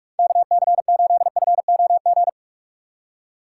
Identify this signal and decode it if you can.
Morse code, heard as a keyed tone, about 46 wpm, 700 Hz. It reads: KC8FOG
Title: USA Call Signs - 6 Characters - 6x Character Spacing - 45wpm